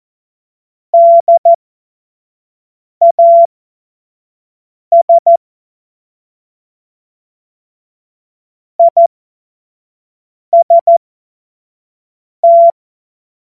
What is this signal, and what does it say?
Morse code, heard as a keyed tone, about 14 wpm, spaced as for 5 wpm, 680 Hz. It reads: DAS IST
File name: Morsekurs / Witze / Witz 15. Juni 2021, 25 F70